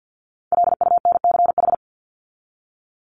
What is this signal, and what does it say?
Morse code, heard as a keyed tone, about 49 wpm, 700 Hz. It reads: LUNCH
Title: Top 1751-2000 Words 50wpm